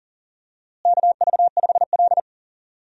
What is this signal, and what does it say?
Morse code, heard as a keyed tone, about 40 wpm, 700 Hz. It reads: KV5L
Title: USA Call Signs - 4 Characters - Repeated 3x - 40wpm